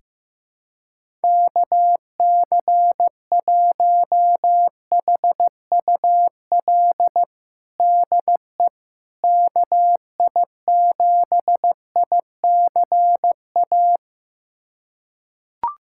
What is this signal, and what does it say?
Morse code, heard as a keyed tone, about 15 wpm, 700 Hz. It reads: KC1HUL DE KI7ICA
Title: USA Callsign DE Callsign - 6 Characters - 15wpm